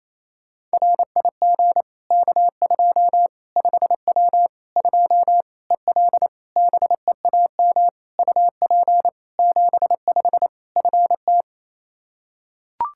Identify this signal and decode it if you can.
Morse code, heard as a keyed tone, about 28 wpm, 700 Hz. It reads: RIG K2 5W 2 EL BEAM UP 75 FT
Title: Practice QSO - Exchange 2 - Info - 28wpm